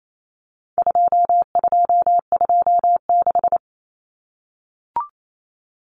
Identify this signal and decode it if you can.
Morse code, heard as a keyed tone, about 28 wpm, 700 Hz. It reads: 2226
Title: Common Cut Numbers - 4-Digit 28wpm